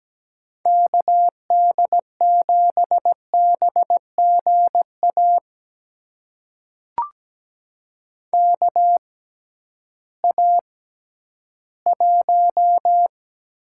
Morse code, heard as a keyed tone, about 17 wpm, spaced as for 11 wpm, 700 Hz. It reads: KD7BGA KA1
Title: USA Call Signs - 6 Characters - 6x Character Spacing - 17wpm